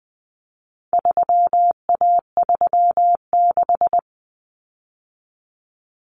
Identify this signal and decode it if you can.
Morse code, heard as a keyed tone, about 20 wpm, 700 Hz. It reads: 3A36
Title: Common Cut Numbers - 4-Digit 20wpm